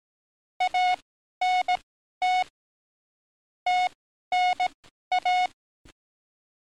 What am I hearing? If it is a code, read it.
Morse code, heard as a keyed tone, about 18 wpm, spaced as for 12 wpm, 730 Hz. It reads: ANT TNA
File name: training / NAVPERS 11049to11060 / NAVPERS 11052 RA RB